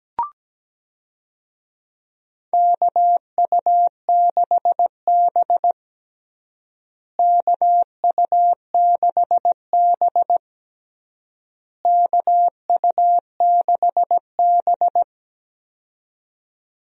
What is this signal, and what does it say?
Morse code, heard as a keyed tone, about 17 wpm, 700 Hz. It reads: KU6B KU6B KU6B
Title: USA Call Signs - 4 Characters - Repeated 3x - 17wpm